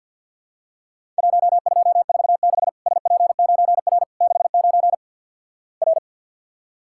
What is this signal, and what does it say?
Morse code, heard as a keyed tone, about 54 wpm, 700 Hz. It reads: 1246 SP9R 69 R
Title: Worked All Europe DX Contest - QTC Exchange with Repeats - 50wpm